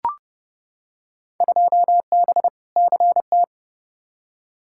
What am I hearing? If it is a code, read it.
Morse code, heard as a keyed tone, about 30 wpm, 700 Hz. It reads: 2B CT
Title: Field Day Exchanges Encore - 30wpm